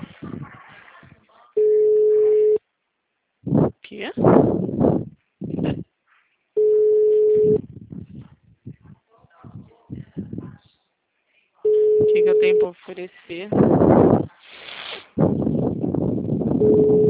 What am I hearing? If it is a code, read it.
Ringback tone (Europe, 425 Hz), with speech.